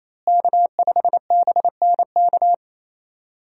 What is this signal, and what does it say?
Morse code, heard as a keyed tone, about 28 wpm, 700 Hz. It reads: K5BNK
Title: USA Call Signs - 5 Characters - Repeated 3x - 28wpm